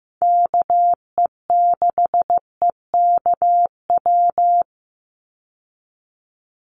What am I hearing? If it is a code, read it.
Morse code, heard as a keyed tone, about 15 wpm, 700 Hz. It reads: KE6EKW
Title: USA Call Signs 15wpm